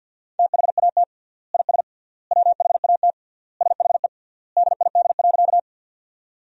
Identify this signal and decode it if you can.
Morse code, heard as a keyed tone, about 50 wpm, 700 Hz. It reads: THAT IS WHAT SHE DID.
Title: 3x Word-Spacing - Sentences from Top 100 Words - 50wpm